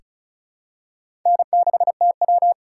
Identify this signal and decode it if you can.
Morse code, heard as a keyed tone, about 36 wpm, 700 Hz. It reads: N6TW
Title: K1USN SST QSO Practice - 35wpm